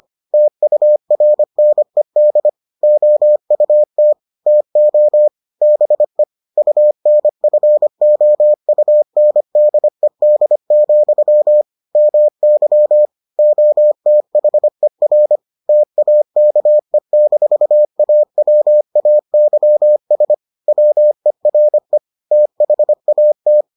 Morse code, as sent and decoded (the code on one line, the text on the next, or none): - ..- .-. -. -.. / --- ..- - / - --- / -... / ..- -. ..-. --- ..- -. -.. -.. --..-- / -- -.-- / --- - .... .-. / - .- -.- . -....- .- .-- .- -.-- ... / .-- . .-. / - .... .- -
TURNED OUT TO BE UNFOUNDED, MY OTHER TAKE-AWAYS WERE THAT